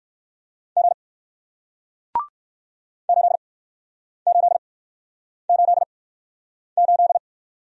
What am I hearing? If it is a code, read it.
Morse code, heard as a keyed tone, about 60 wpm, spaced as for 9 wpm, 700 Hz. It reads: G 8888